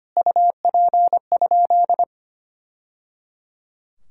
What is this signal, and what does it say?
Morse code, heard as a keyed tone, about 25 wpm, 700 Hz. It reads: UP?
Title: Sentences from Top 1000 Words - Review Part 6 - 25wpm